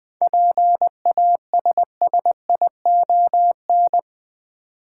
Morse code, heard as a keyed tone, about 20 wpm, 700 Hz. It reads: PASSION
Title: Seven Letter Words - 20wpm